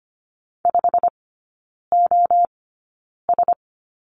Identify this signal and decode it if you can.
Morse code, heard as a keyed tone, about 25 wpm, spaced as for 9 wpm, 700 Hz. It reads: 5OS
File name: Lesson 15e - Number 5 - Three Characters 25wpm